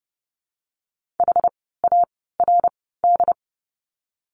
Morse code, heard as a keyed tone, about 30 wpm, spaced as for 17 wpm, 700 Hz. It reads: HARD
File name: Lesson 16f - Letter c - Words 30wpm